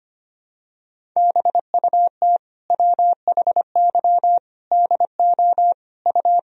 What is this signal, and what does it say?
Morse code, heard as a keyed tone, about 25 wpm, 700 Hz. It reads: BUT WHY DO U
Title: Sentences from Top 200 Words and Abbreviations 25wpm